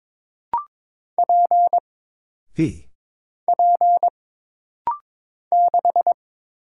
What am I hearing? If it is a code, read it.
Morse code, heard as a keyed tone, about 22 wpm, 700 Hz. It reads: P P 6
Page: This rapid-fire timing is geared to push your ICR (Instant Character Recognition) skills.